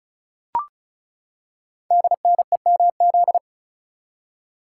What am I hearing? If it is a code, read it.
Morse code, heard as a keyed tone, about 36 wpm, 700 Hz. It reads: DNEMZ
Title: Code Groups - 5 Letters - 35wpm